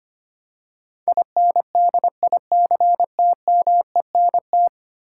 Morse code, heard as a keyed tone, about 25 wpm, 700 Hz. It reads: INDICTMENT
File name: Ten Letter Words - 25wpm